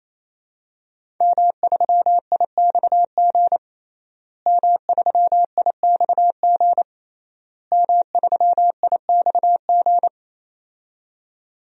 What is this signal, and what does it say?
Morse code, heard as a keyed tone, about 27 wpm, 700 Hz. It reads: M3IXG M3IXG M3IXG
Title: DXCC-100 Call Signs - 5 Characters - Repeated 3x - 28wpm